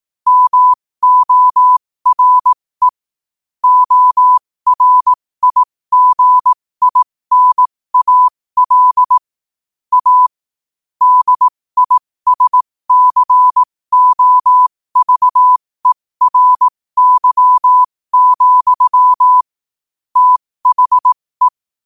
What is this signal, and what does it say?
Morse code, heard as a keyed tone, about 18 wpm, spaced as for 15 wpm, 1000 Hz. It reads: MORE ORIGINAL A DISCOVERY, THE
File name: qotdfiles / 15wpm